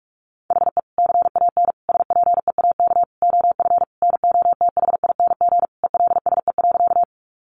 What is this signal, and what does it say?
Morse code, heard as a keyed tone, about 44 wpm, 700 Hz. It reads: HE CAN SPEAK OF NOTHING ELSE.
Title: Sentences from Top 1100 Words 45wpm